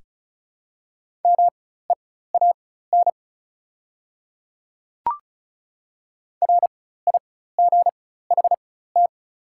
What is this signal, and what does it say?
Morse code, heard as a keyed tone, about 36 wpm, spaced as for 16 wpm, 700 Hz. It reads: MEAN RIGHT
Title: 4x Character Spacing - Top 200 Words - 35wpm